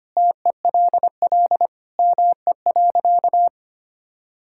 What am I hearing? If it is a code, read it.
Morse code, heard as a keyed tone, about 25 wpm, 700 Hz. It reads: TELL ME.
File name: practice / 25wpm / Sentences with 3-Word Phrases - 25wpm